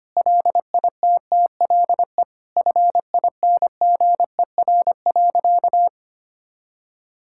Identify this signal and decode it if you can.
Morse code, heard as a keyed tone, about 25 wpm, 700 Hz. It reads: LITTLE FINGER.
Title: Sentences from Top 1600 Words 25wpm